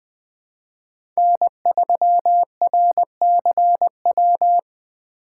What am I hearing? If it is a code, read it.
Morse code, heard as a keyed tone, about 21 wpm, 700 Hz. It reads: N3RCW
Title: 2023 POTA Hunter Call Signs - 20wpm